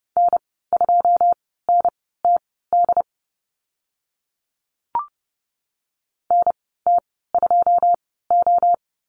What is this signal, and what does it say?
Morse code, heard as a keyed tone, about 30 wpm, spaced as for 17 wpm, 700 Hz. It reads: N2NTD NT2O
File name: Lesson 14g - Number 2 - Call Signs 30wpm